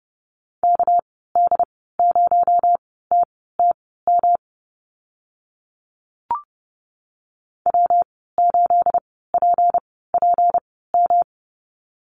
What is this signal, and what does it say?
Morse code, heard as a keyed tone, about 30 wpm, spaced as for 17 wpm, 700 Hz. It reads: KD0TTM W8PPM
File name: Lesson 37g - Letter x - Call Signs 30wpm